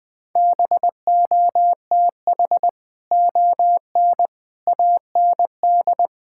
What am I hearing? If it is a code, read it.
Morse code, heard as a keyed tone, about 21 wpm, 700 Hz. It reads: BOTH ON AND
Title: Sets of 6 Words - 20wpm